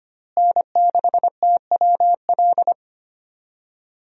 Morse code, heard as a keyed tone, about 25 wpm, 700 Hz. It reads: N6TWL
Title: USA Call Signs - 5 Characters - Repeated 3x - 25wpm